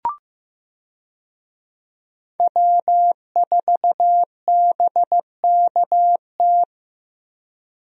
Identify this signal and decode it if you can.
Morse code, heard as a keyed tone, about 15 wpm, 700 Hz. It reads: W4BKT
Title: USA Call Signs - 5 Characters - Repeated 2x - 15wpm